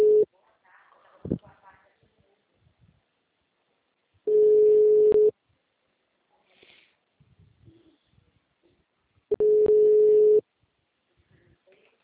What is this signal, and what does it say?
Ringback tone (Europe, 425 Hz), with speech.